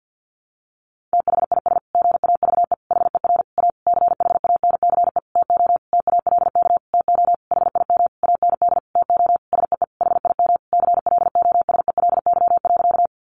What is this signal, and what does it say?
Morse code, heard as a keyed tone, about 50 wpm, 700 Hz. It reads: THIS GAVE HER A CHANCE TO TALK TO HIM AND TO SEE HIM CLOSELY.
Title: Sentences from Top 1400 Words 50wpm